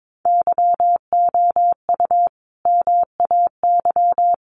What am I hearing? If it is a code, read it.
Morse code, heard as a keyed tone, about 22 wpm, 700 Hz. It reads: YOU MAY
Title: Sentences from Top 1400 Words 22wpm